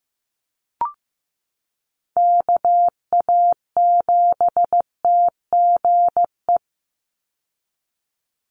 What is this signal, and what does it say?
Morse code, heard as a keyed tone, about 15 wpm, 700 Hz. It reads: KA7TGE
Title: USA Call Signs 15wpm